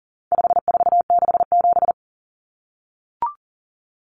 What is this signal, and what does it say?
Morse code, heard as a keyed tone, about 40 wpm, 700 Hz. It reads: 5467
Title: Common Cut Numbers - 4-Digit 40wpm